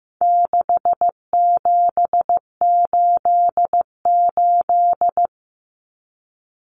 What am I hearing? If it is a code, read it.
Morse code, heard as a keyed tone, about 15 wpm, 700 Hz. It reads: 6788
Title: Numbers - 4-Digit 15wpm